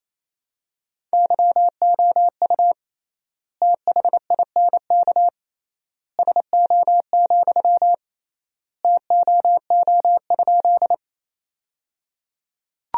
Morse code, heard as a keyed tone, about 27 wpm, 700 Hz. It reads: YOU THINK SO, TOO?
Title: 3x Word-Spacing - Sentences from Top 200 Words - 28wpm